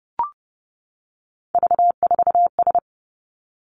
Morse code, heard as a keyed tone, about 30 wpm, 700 Hz. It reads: V4S
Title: Three Challenging Letter-Number Combinations 30wpm